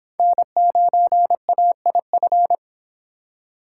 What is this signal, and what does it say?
Morse code, heard as a keyed tone, about 25 wpm, 700 Hz. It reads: N9AIF